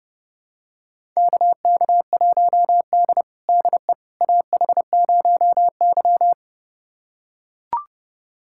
Morse code, heard as a keyed tone, about 30 wpm, 700 Hz. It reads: KK1D DE AH0Y
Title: USA Callsign DE Callsign - 4 Characters - 30wpm